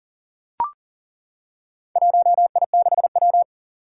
Morse code, heard as a keyed tone, about 37 wpm, 700 Hz. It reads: 1I6W
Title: Four Challenging Letter-Number Combinations 40wpm